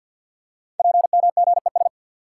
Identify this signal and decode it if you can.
Morse code, heard as a keyed tone, about 50 wpm, 690 Hz. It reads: PMGES
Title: Code Groups - 5 Letters - 50wpm